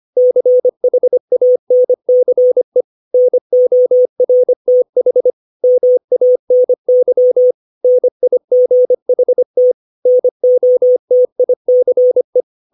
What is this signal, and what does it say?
Morse code, heard as a keyed tone, about 25 wpm, 500 Hz. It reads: CHANCE NORTH MANY NIGHT NOTICE
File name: 632 words / 632 Words 25WPM